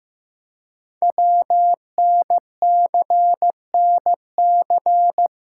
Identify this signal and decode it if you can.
Morse code, heard as a keyed tone, about 15 wpm, 700 Hz. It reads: WNCNC
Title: Code Groups - 5 Letters - 15wpm